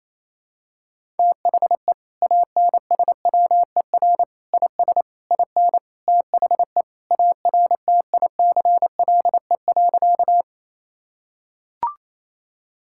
Morse code, heard as a keyed tone, about 27 wpm, 700 Hz. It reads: THE ANSWER IS IN THE ARTICLE.
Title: Sentences from Top 1000 Words - Review Part 5 - 28wpm